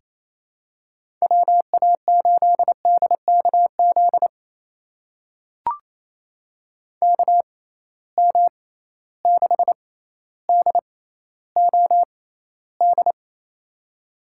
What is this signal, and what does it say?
Morse code, heard as a keyed tone, about 28 wpm, spaced as for 14 wpm, 700 Hz. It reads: WA8DKZ KM6DOD
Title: USA Call Signs - 6 Characters - 6x Character Spacing - 28wpm